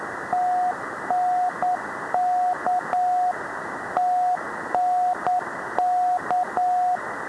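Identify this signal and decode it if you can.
Morse code, heard as a keyed tone, about 9 wpm, 720 Hz, with radio noise. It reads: TNK TNK
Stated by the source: Morse numbers station.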